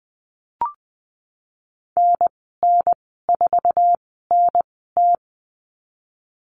Morse code, heard as a keyed tone, about 19 wpm, spaced as for 14 wpm, 700 Hz. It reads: NN4NT
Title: 179 Lesson 29g - Slash - Call Signs 20wpm